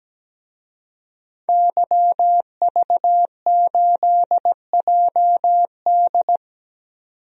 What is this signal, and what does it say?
Morse code, heard as a keyed tone, about 17 wpm, 700 Hz. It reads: YV8JD
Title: DXCC-100 Call Signs - 5 Characters - 17wpm